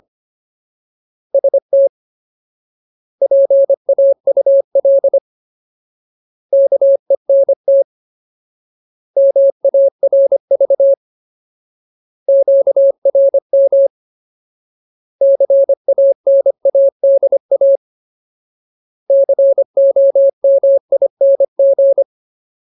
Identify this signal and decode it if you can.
Morse code, heard as a keyed tone, about 25 wpm, 550 Hz. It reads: ST PAUL KENT MARV QRM CANADA COMING
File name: Ultimate Word List / 25WPM Ultimate